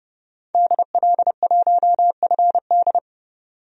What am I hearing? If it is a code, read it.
Morse code, heard as a keyed tone, about 30 wpm, 700 Hz. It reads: DL1FD
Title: DXCC-100 Call Signs - 5 Characters - 30wpm